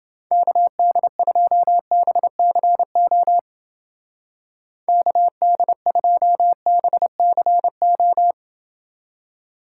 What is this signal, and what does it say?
Morse code, heard as a keyed tone, about 30 wpm, 700 Hz. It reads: KD2BCO KD2BCO